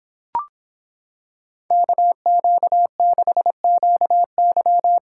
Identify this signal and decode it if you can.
Morse code, heard as a keyed tone, about 25 wpm, 700 Hz. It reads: KQ6QY